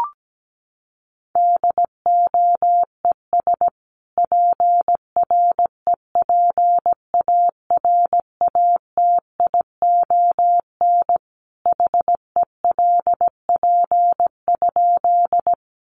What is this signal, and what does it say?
Morse code, heard as a keyed tone, about 17 wpm, 700 Hz. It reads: DOES PREPARATION HELP?
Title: Sentences from Top 1400 Words 17wpm